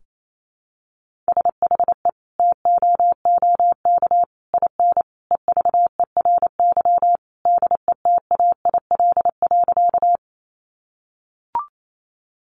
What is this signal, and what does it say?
Morse code, heard as a keyed tone, about 28 wpm, 700 Hz. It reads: SHE TOOK IN EVERY DETAIL.
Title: Sentences from Top 1500 Words 28wpm